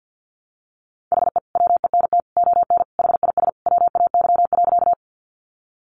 Morse code, heard as a keyed tone, about 48 wpm, 700 Hz. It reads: HE WENT ON HIS WAY.
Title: Sentences from Top 200 Words 50wpm